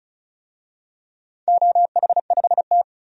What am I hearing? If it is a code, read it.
Morse code, heard as a keyed tone, about 34 wpm, 700 Hz. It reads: OH5T
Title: DXCC-100 Call Signs - 4 Characters - 35wpm